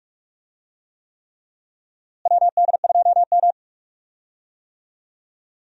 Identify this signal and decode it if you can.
Morse code, heard as a keyed tone, about 46 wpm, 700 Hz. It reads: WD2M
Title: USA Call Signs - 4 Characters - Repeated 3x - 45wpm